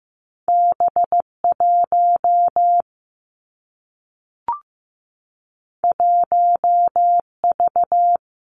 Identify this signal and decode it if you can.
Morse code, heard as a keyed tone, about 15 wpm, 700 Hz. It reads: B 1 1 V
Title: Two Challenging Letter-Number Combinations 15wpm